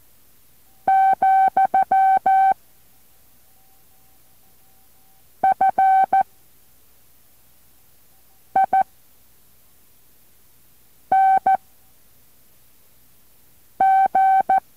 Morse code, heard as a keyed tone, about 14 wpm, 770 Hz. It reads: , F I N G